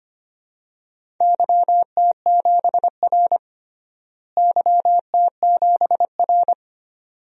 Morse code, heard as a keyed tone, about 25 wpm, 700 Hz. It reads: YT7R YT7R
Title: DXCC-100 Call Signs - 4 Characters - Repeated 2x - 25wpm